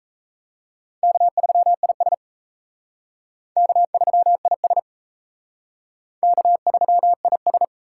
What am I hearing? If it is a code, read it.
Morse code, heard as a keyed tone, about 34 wpm, 700 Hz. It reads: K3IS K3IS K3IS
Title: Speed Racing - USA 1x2 Call Signs 28wpm